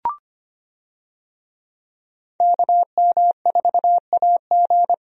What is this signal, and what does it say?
Morse code, heard as a keyed tone, about 25 wpm, 700 Hz. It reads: KM4AG
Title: USA Call Signs - 5 Characters - 25wpm